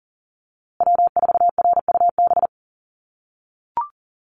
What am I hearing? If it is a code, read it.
Morse code, heard as a keyed tone, about 40 wpm, 700 Hz. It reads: W4RUB
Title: USA Call Signs Repeated 2x 40wpm